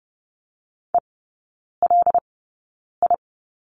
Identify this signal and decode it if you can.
Morse code, heard as a keyed tone, about 30 wpm, spaced as for 9 wpm, 700 Hz. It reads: ELI
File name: Lesson 18e - Letter m - Three Characters 30wpm